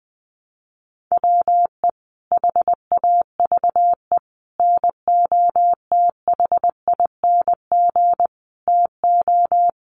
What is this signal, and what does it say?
Morse code, heard as a keyed tone, about 20 wpm, 700 Hz. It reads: WE HAVE NOTHING TO
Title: Sentences from Top 1400 Words 20wpm